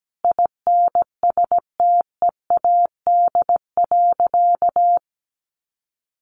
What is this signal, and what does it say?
Morse code, heard as a keyed tone, about 17 wpm, 700 Hz. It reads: INSTEAD.
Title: Sentences from Top 800 Words 17wpm